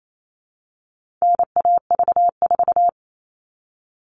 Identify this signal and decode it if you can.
Morse code, heard as a keyed tone, about 28 wpm, 700 Hz. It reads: NAV4
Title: All Cut Numbers - 4-Digit 28wpm